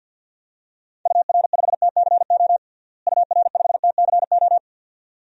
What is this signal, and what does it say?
Morse code, heard as a keyed tone, about 53 wpm, 700 Hz. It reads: UR5TCO UR5TCO
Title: DXCC-100 Call Signs - 6 Characters - Repeated 2x - 50wpm